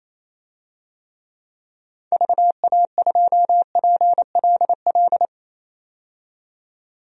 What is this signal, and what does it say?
Morse code, heard as a keyed tone, about 28 wpm, 700 Hz. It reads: VA2PLL
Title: DXCC-100 Call Signs - 6 Characters - 28wpm